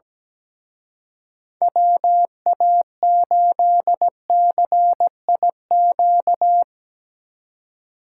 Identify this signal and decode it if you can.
Morse code, heard as a keyed tone, about 19 wpm, 700 Hz. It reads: WA8CIQ
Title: Speed Racing - USA Call Signs 17wpm